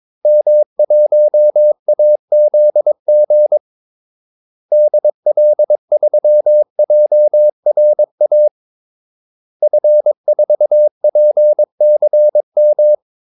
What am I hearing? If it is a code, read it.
Morse code, heard as a keyed tone, about 22 wpm, 600 Hz. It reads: M1AZG DL3JRA F4PCM
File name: Callsigns-22wpm